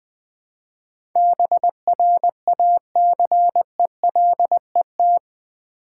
Morse code, heard as a keyed tone, about 20 wpm, 700 Hz. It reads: BRACELET
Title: Easy Words with Hints - 20wpm